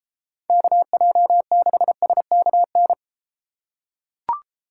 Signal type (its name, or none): Morse code